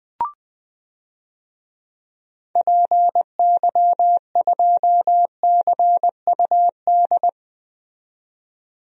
Morse code, heard as a keyed tone, about 20 wpm, 700 Hz. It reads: PY2CUD
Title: DXCC-100 Call Signs - 6 Characters - 20wpm